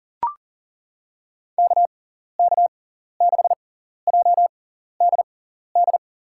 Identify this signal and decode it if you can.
Morse code, heard as a keyed tone, about 37 wpm, spaced as for 14 wpm, 700 Hz. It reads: KK6JDD
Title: USA Call Signs - 6 Characters - 6x Character Spacing - 40wpm